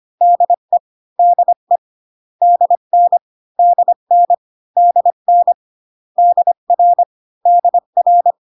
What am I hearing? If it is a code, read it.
Morse code, heard as a keyed tone, about 25 wpm, 700 Hz. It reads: DE DE DN DN DN DR DR